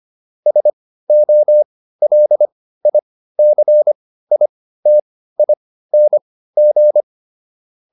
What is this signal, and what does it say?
Morse code, heard as a keyed tone, about 25 wpm, spaced as for 15 wpm, 600 Hz. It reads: SOLICITING